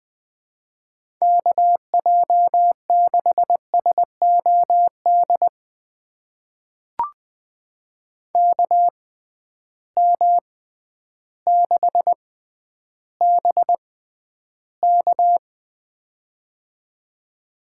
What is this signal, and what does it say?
Morse code, heard as a keyed tone, about 20 wpm, spaced as for 11 wpm, 700 Hz. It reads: KJ6SOD KM6BK
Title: USA Call Signs - 6x Character Spacing - 20wpm